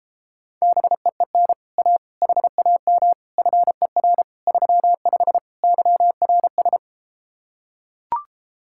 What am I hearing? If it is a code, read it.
Morse code, heard as a keyed tone, about 33 wpm, 700 Hz. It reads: BEEN A HAM FER 35 YRS